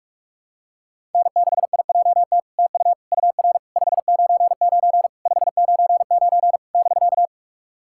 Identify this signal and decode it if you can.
Morse code, heard as a keyed tone, about 44 wpm, 700 Hz. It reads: N6IJT TU UR 599 599 <BK>
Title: SOTA QSO Practice 45wpm